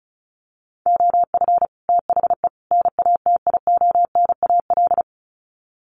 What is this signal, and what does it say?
Morse code, heard as a keyed tone, about 34 wpm, 700 Hz. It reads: OF THE NATIONAL
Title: Sets of 3 Words 35wpm